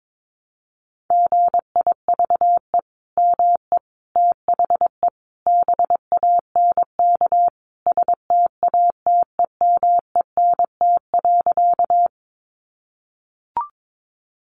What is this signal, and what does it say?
Morse code, heard as a keyed tone, about 22 wpm, 700 Hz. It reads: GIVE ME THE BANK STATEMENT.
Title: Sentences from Top 1200 Words 22wpm